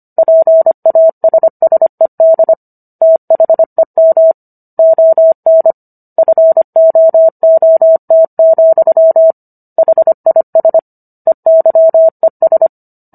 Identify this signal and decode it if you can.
Morse code, heard as a keyed tone, about 25 wpm, 650 Hz. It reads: PASSED THEM ON FOOT, HIS EYES